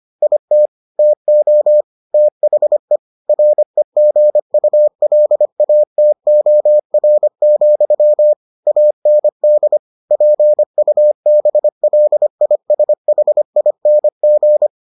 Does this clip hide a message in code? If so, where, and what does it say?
Yes: Morse code from 0.2 to 14.7 s, heard as a keyed tone, about 25 wpm, 600 Hz. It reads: IT TO THE REGULATOR, AND PUBLISHING